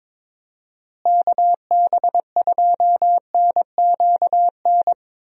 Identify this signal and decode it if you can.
Morse code, heard as a keyed tone, about 22 wpm, 700 Hz. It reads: KB2NQN
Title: USA Call Signs - 6 Characters - Repeated 2x - 22wpm